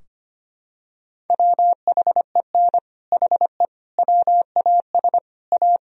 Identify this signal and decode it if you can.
Morse code, heard as a keyed tone, about 25 wpm, 700 Hz. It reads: WHEN HE WAS A
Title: Sentences from Top 1000 Words - Review Part 3 - 25wpm